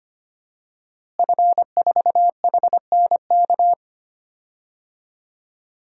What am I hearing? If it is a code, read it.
Morse code, heard as a keyed tone, about 25 wpm, 700 Hz. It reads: F4HNK
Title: DXCC-100 Call Signs - 5 Characters - Repeated 2x - 25wpm